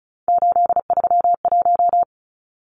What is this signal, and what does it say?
Morse code, heard as a keyed tone, about 34 wpm, 700 Hz. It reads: 831
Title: Numbers - 3-Digit 35wpm